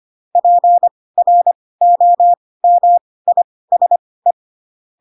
Morse code, heard as a keyed tone, about 25 wpm, spaced as for 18 wpm, 700 Hz. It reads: PROMISE